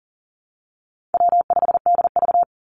Morse code, heard as a keyed tone, about 40 wpm, 700 Hz. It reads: W5DV
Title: Call Signs Repeated 3x - 40wpm